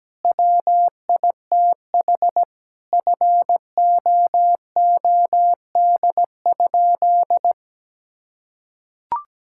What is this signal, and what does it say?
Morse code, heard as a keyed tone, about 17 wpm, 700 Hz. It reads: WITH FOOD?
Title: Sentences from Top 1000 Words - Review Part 5 - 17wpm